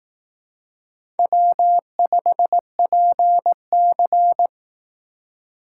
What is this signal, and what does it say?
Morse code, heard as a keyed tone, about 18 wpm, 700 Hz. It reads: W5PC
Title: Speed Racing - USA 1x2 Call Signs 15wpm